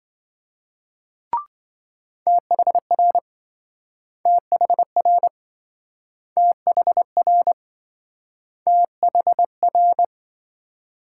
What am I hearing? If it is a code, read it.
Morse code, heard as a keyed tone, about 27 wpm, 700 Hz. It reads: THR THR THR THR